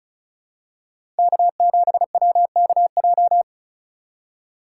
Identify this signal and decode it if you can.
Morse code, heard as a keyed tone, about 34 wpm, 700 Hz. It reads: K7WKJ
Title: USA Call Signs - 5 Characters - 2x Character Spacing - 35wpm